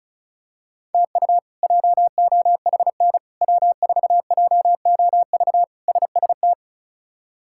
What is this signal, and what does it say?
Morse code, heard as a keyed tone, about 36 wpm, 700 Hz. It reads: TU JOHN W4JOV SST
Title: K1USN SST QSO Practice - 35wpm